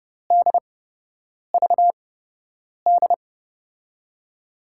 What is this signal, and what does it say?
Morse code, heard as a keyed tone, about 32 wpm, spaced as for 8 wpm, 700 Hz. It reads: DVD
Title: USA Call Signs - 5 Characters - 8x Character Spacing - 30wpm